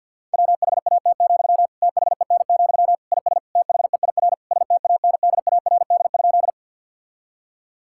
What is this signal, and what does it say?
Morse code, heard as a keyed tone, about 50 wpm, 700 Hz. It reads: WHAT, THEN, IS THEIR STANDARD?
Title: Sentences from Top 1000 Words - Review Part 6 - 50wpm